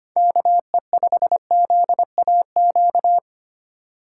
Morse code, heard as a keyed tone, about 25 wpm, 700 Hz. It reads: KE5ZAQ
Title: USA Call Signs - 3x Character Spacing - 25wpm